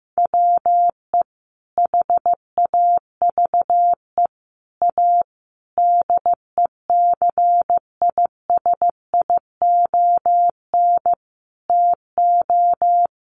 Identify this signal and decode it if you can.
Morse code, heard as a keyed tone, about 15 wpm, 700 Hz. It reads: WE HAVE A DECISION TO
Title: Sentences from Top 1000 Words 15wpm